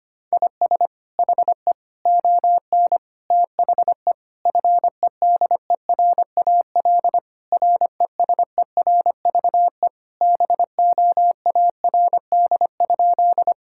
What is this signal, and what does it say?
Morse code, heard as a keyed tone, about 25 wpm, 700 Hz. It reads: IS HE ON THE FEDERAL RESERVE BOARD?
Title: Sentences from Top 1000 Words - Review Part 6 - 25wpm